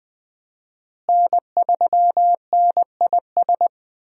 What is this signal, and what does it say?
Morse code, heard as a keyed tone, about 20 wpm, 700 Hz. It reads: N3NIS
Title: USA Call Signs - 5 Characters - Repeated 3x - 20wpm